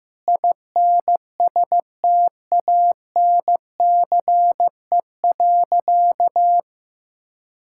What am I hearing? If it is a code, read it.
Morse code, heard as a keyed tone, about 15 wpm, 700 Hz. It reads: INSTANCE.
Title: Sentences from Top 1000 Words - Review Part 5 - 15wpm